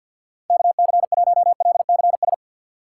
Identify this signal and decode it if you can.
Morse code, heard as a keyed tone, about 47 wpm, 700 Hz. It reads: KC1LCS
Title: USA Call Signs - 6 Characters - 4x Character Spacing - 50wpm